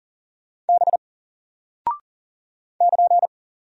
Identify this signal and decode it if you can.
Morse code, heard as a keyed tone, about 40 wpm, 700 Hz. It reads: B <KN>
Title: Single Character (Letter:Number:Punctuation:Prosign) Alt 40wpm